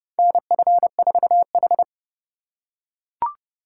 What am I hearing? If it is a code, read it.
Morse code, heard as a keyed tone, about 30 wpm, 700 Hz. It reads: NF4H
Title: USA Call Signs - 4 Characters - 8x Character Spacing - 30wpm